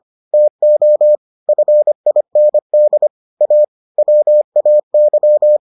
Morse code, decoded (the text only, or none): TO FIND A WAY